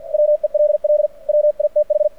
Morse code, heard as a keyed tone, about 51 wpm, 600 Hz. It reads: WELL WITH
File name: archived / CRSnet